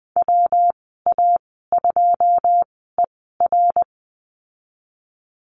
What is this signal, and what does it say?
Morse code, heard as a keyed tone, about 20 wpm, spaced as for 14 wpm, 700 Hz. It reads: WA2ER
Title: 128 Lesson 21g - Number 6 - Call Signs 20wpm